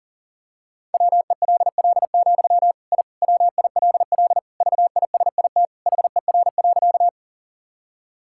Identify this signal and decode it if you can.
Morse code, heard as a keyed tone, about 40 wpm, 700 Hz. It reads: WELL, I WILL VISIT HER.
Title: Sentences from Top 1000 Words - Review Part 6 - 40wpm